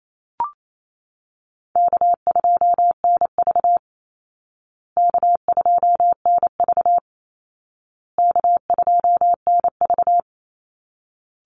Morse code, heard as a keyed tone, about 28 wpm, 700 Hz. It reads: K2NV K2NV K2NV
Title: Easy Call Signs Repeated 3x - 28wpm